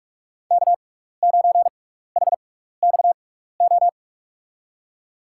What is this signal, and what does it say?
Morse code, heard as a keyed tone, about 44 wpm, spaced as for 15 wpm, 700 Hz. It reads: K9HXO
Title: USA Call Signs - 6x Character Spacing - 45wpm